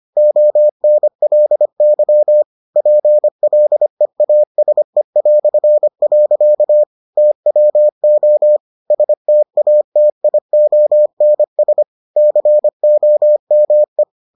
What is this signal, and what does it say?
Morse code, heard as a keyed tone, about 25 wpm, 600 Hz. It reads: ONLY PLEASE". TWO STATIONS COME